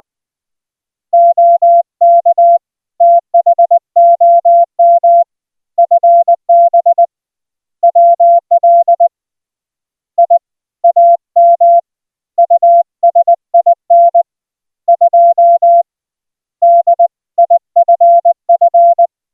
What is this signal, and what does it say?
Morse code, heard as a keyed tone, about 20 wpm, 690 Hz. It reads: OK THOM FB WL I AM USIN 2 DIFF